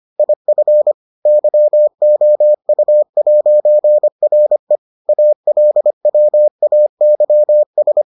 Morse code, heard as a keyed tone, about 25 wpm, 600 Hz. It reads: IF YOU'RE ALWAYS